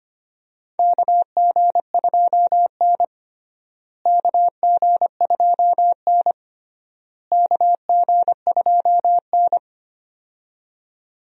Morse code, heard as a keyed tone, about 25 wpm, 700 Hz. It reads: KG2N KG2N KG2N
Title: USA Call Signs - 4 Characters - Repeated 3x - 25wpm